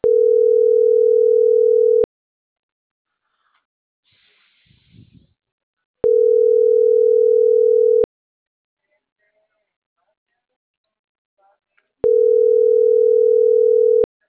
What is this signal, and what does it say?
Ringback tone (North America), 440 + 480 Hz.